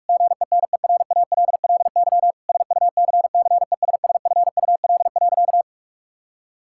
Morse code, heard as a keyed tone, about 41 wpm, 700 Hz. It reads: GENERALLY SUCCESSFUL.
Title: Sentences from Top 1000 Words - Review Part 6 - 45wpm